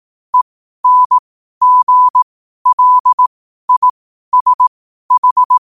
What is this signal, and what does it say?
Morse code, heard as a keyed tone, about 18 wpm, spaced as for 12 wpm, 1000 Hz. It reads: ENGLISH
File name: 12wpm